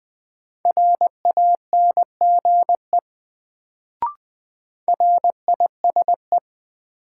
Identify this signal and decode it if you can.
Morse code, heard as a keyed tone, about 21 wpm, 700 Hz. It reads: RANGE RISE
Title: Sentences from Top 900 Words - New Word Drill - 20wpm